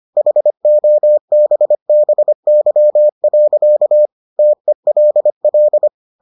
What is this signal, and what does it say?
Morse code, heard as a keyed tone, about 25 wpm, 600 Hz. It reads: HOBBY. TELL